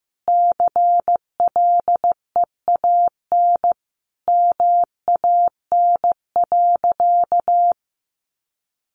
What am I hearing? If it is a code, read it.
Morse code, heard as a keyed tone, about 15 wpm, 700 Hz. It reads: CLEAN MAN.
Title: Sentences from Top 1400 Words 15wpm